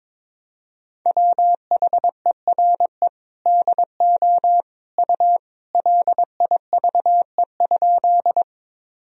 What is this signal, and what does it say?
Morse code, heard as a keyed tone, about 22 wpm, 700 Hz. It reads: WHERE DO U LIVE?
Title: Sentences from Top 200 Words and Abbreviations 22wpm